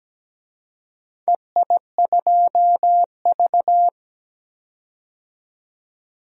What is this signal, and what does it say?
Morse code, heard as a keyed tone, about 17 wpm, 700 Hz. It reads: EI2V
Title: DXCC-100 Call Signs - 4 Characters - 17wpm